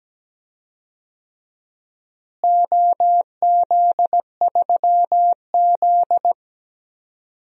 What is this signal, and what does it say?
Morse code, heard as a keyed tone, about 17 wpm, 700 Hz. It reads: OZ3Z
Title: DXCC-100 Call Signs - 4 Characters - Repeated 3x - 17wpm